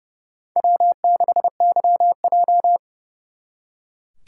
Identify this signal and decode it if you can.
Morse code, heard as a keyed tone, about 36 wpm, 700 Hz. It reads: W6YJ
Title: Speed Racing - USA 1x2 Call Signs 30wpm